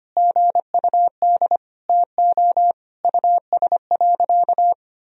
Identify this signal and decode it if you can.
Morse code, heard as a keyed tone, about 25 wpm, 700 Hz. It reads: GUD TO US.
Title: Sentences from Top 200 Words and Abbreviations 25wpm